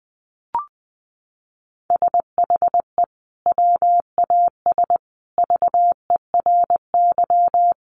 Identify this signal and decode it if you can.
Morse code, heard as a keyed tone, about 20 wpm, 700 Hz. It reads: SHE WAS VERY
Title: Sentences from Top 900 Words 20wpm